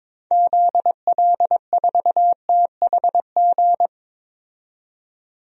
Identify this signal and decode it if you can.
Morse code, heard as a keyed tone, about 22 wpm, 700 Hz. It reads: ZL4THG
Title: DXCC-100 Call Signs - 6 Characters - 22wpm